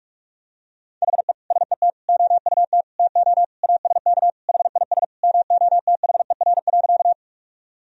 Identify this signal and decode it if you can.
Morse code, heard as a keyed tone, about 46 wpm, 700 Hz. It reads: HE SET OUT TO ASK HIS MOTHER.
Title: Sentences from Top 1000 Words - Review Part 1 - 45wpm